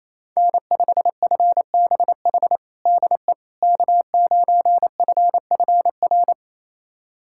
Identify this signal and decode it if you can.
Morse code, heard as a keyed tone, about 28 wpm, 700 Hz. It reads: N5FBH DE K9FFR
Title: USA Callsign DE Callsign - 5 Characters - 28wpm